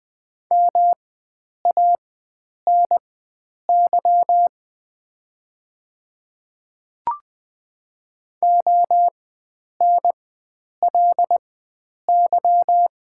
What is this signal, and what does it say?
Morse code, heard as a keyed tone, about 20 wpm, spaced as for 9 wpm, 700 Hz. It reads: MANY ONLY
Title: 4x Character Spacing - Top 500 Words - 20wpm